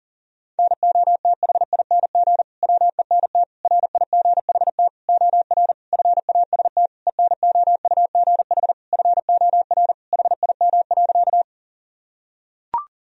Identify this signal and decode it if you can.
Morse code, heard as a keyed tone, about 37 wpm, 700 Hz. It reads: NOTHING WENT RIGHT OR FAST ENOUGH FOR HIM.
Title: Sentences from Top 1000 Words - Review Part 2 - 40wpm